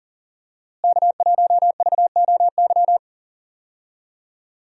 Morse code, heard as a keyed tone, about 40 wpm, 700 Hz. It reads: K1VOY
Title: USA Call Signs - 5 Characters - Repeated 2x - 40wpm